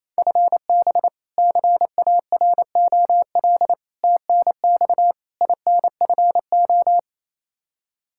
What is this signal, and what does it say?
Morse code, heard as a keyed tone, about 28 wpm, 700 Hz. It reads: FB CAROL TNX INFO
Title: K1USN SST QSO Practice 28wpm